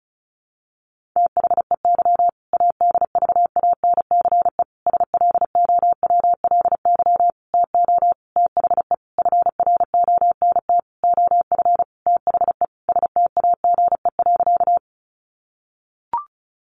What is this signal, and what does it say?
Morse code, heard as a keyed tone, about 34 wpm, 700 Hz. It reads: THEY ADVANCE SLOWLY TO THE FRONT OF THE STAGE.
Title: Sentences from Top 1400 Words 35wpm